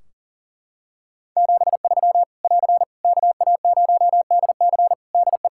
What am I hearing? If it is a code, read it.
Morse code, heard as a keyed tone, about 40 wpm, 700 Hz. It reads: 73 <AR> KA0DC DE
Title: Practice QSO - Exchange 4 - Ending - 40wpm